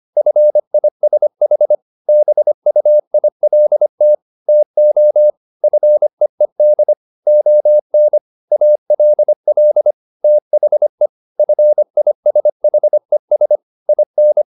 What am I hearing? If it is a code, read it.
Morse code, heard as a keyed tone, about 25 wpm, 600 Hz. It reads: FISH BUILT TO FEED ON ALL THE FISHES IN